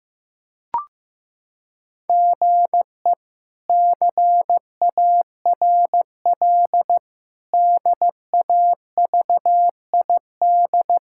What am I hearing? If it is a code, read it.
Morse code, heard as a keyed tone, about 15 wpm, 700 Hz. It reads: GE CARL DAVID
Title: K1USN SST QSO Practice - 15wpm